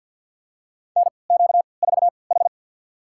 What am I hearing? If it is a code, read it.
Morse code, heard as a keyed tone, about 52 wpm, spaced as for 29 wpm, 700 Hz. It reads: NQ4H
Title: USA Call Signs - 3x Character Spacing - 50wpm